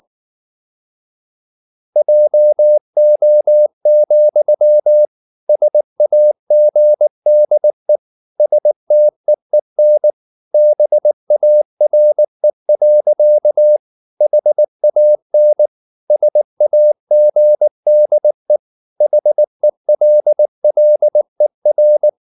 Morse code, heard as a keyed tone, about 19 wpm, 600 Hz. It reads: JO, SAGDE STEEN BARE. HAN SAGDE HELLER
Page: Ferie 19 wpm